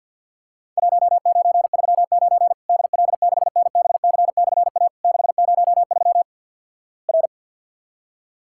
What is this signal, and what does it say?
Morse code, heard as a keyed tone, about 50 wpm, 700 Hz. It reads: 1939 DL6NBC/A 603 R
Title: Worked All Europe DX Contest - QTC Exchange - 50wpm